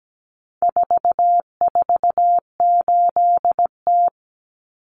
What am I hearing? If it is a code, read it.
Morse code, heard as a keyed tone, about 17 wpm, 700 Hz. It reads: 448T
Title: Common Cut Numbers - 4-Digit 17wpm